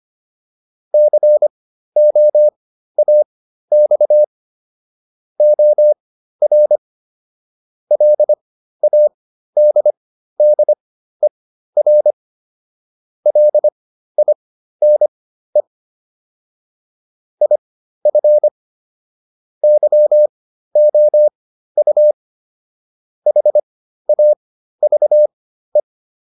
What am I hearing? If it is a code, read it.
Morse code, heard as a keyed tone, about 25 wpm, spaced as for 13 wpm, 600 Hz. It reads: COAX OR LADDER LINE IF YOU HAVE